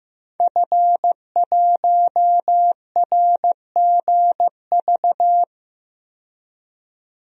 Morse code, heard as a keyed tone, about 15 wpm, 700 Hz. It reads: F1RGV
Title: DXCC-100 Call Signs - 5 Characters - 15wpm